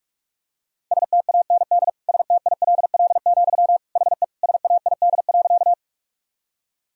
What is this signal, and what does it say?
Morse code, heard as a keyed tone, about 46 wpm, 700 Hz. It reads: STAND STILL, HE SAID.
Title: Sentences from Top 1000 Words - Review Part 1 - 45wpm